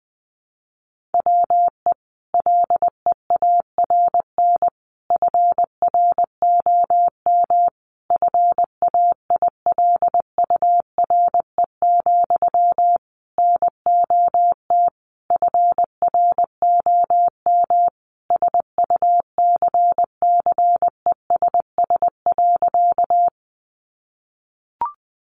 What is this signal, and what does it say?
Morse code, heard as a keyed tone, about 20 wpm, 700 Hz. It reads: WE LEARN FROM FAILURE, NOT FROM SUCCESS.
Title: Sentences from Top 900 Words 20wpm